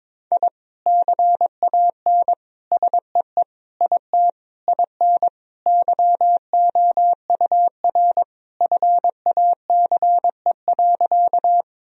Morse code, heard as a keyed tone, about 22 wpm, 700 Hz. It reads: I CAN SEE IT IN YOUR FACE.
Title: Sentences from Top 1000 Words - Review Part 2 - 22wpm